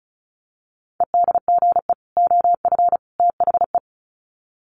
Morse code, heard as a keyed tone, about 34 wpm, 700 Hz. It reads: EDGE OF THE
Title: Sets of 3 Words 35wpm